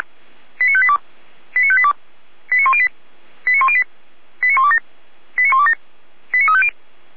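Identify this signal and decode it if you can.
Five-tone selective call (ZVEI), heard as quick runs of five tones, tones of about 70 ms. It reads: 89582 89582 89108 89108 89127 89127 89350
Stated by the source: Five-tone ZVEI selective call received on SDR.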